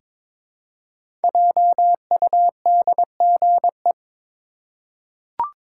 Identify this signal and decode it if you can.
Morse code, heard as a keyed tone, about 22 wpm, 700 Hz. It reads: JUDGE
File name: 5 Letter Words - First 3 Letters Given - 22wpm